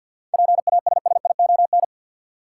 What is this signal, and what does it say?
Morse code, heard as a keyed tone, about 52 wpm, 700 Hz. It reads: PASSION
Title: Seven Letter Words - 50wpm